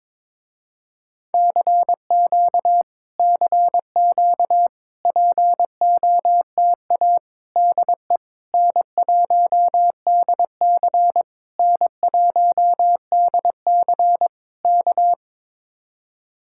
Morse code, heard as a keyed tone, about 22 wpm, 700 Hz. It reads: CQ CQ POTA DE N1DC N1DC K
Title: Speed-Racing Common QSO Phrases - 22wpm